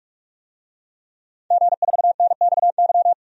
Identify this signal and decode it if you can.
Morse code, heard as a keyed tone, about 46 wpm, 700 Hz. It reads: G4NXY
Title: DXCC-100 Call Signs - 5 Characters - 45wpm